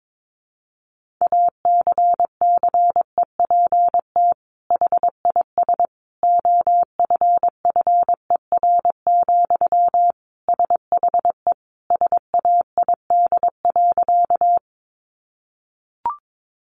Morse code, heard as a keyed tone, about 22 wpm, 700 Hz. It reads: ACCEPT HIS OFFER, SHE SAID.
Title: Sentences from Top 1300 Words 22wpm